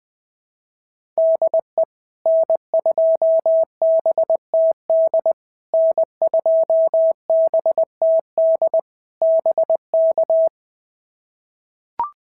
Morse code, heard as a keyed tone, about 20 wpm, 650 Hz. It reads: DE N2BTD N2BTD BK
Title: POTA QSO Practice - 20wpm